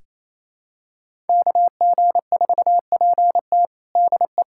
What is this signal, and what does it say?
Morse code, heard as a keyed tone, about 28 wpm, 700 Hz. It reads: KG4PT DE
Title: USA Callsign DE Callsign - 5 Characters - 28wpm